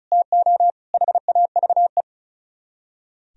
Morse code, heard as a keyed tone, about 46 wpm, spaced as for 41 wpm, 700 Hz. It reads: TO HAVE
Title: Speed-Racing - Sets of 5 Words 35wpm